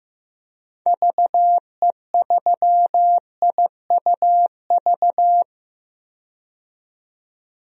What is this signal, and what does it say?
Morse code, heard as a keyed tone, about 15 wpm, 700 Hz. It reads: VE3IUV
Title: DXCC-100 Call Signs - 6 Characters - 15wpm